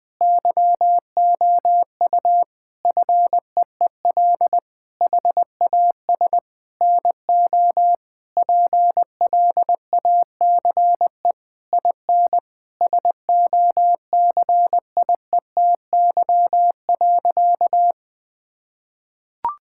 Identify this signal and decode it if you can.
Morse code, heard as a keyed tone, about 20 wpm, 700 Hz. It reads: YOU FEEL HAS NO PLACE IN SOCIETY.
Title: Sentences from Top 1600 Words 20wpm